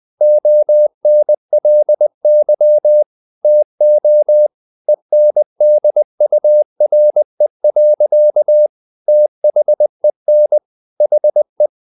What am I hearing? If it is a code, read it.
Morse code, heard as a keyed tone, about 20 wpm, 600 Hz. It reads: ONLY TO ENDURE. THEN HE
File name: the old man and the sea / the old man and the sea part 1 20wpm